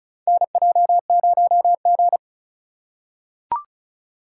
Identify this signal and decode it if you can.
Morse code, heard as a keyed tone, about 36 wpm, 700 Hz. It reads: NJ0G
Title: USA Call Signs - 4x Character Spacing - 35wpm